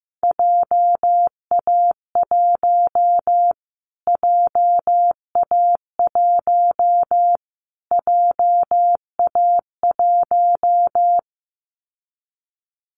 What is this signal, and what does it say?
Morse code, heard as a keyed tone, about 15 wpm, 700 Hz. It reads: JA1 JA1 JA1
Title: Callsign Prefixes for Top 25 DXCC - Repeated 3x - 15wpm